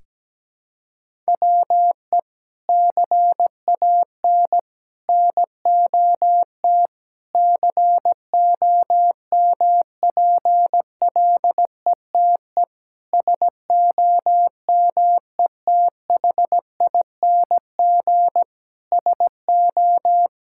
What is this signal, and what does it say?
Morse code, heard as a keyed tone, about 17 wpm, 700 Hz. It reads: WE CAN NOT COMPLETE SOMETHING SO
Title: Sentences from Top 1000 Words - Review Part 6 - 17wpm